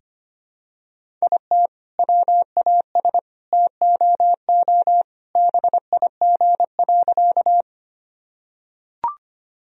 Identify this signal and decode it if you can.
Morse code, heard as a keyed tone, about 25 wpm, 700 Hz. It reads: IT WAS TOO BIG.
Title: Sentences from Top 1000 Words - Review Part 1 - 25wpm